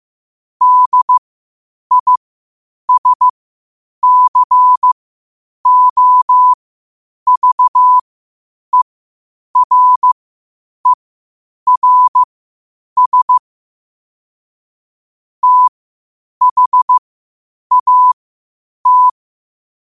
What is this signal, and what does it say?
Morse code, heard as a keyed tone, about 15 wpm, spaced as for 8 wpm, 1000 Hz. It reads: DISCOVERERS THAT
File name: qotdfiles / 8wpm